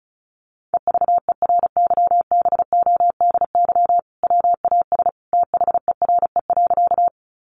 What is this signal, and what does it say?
Morse code, heard as a keyed tone, about 34 wpm, 700 Hz. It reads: EVERYBODY WAS THERE.
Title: Sentences from Top 1400 Words 35wpm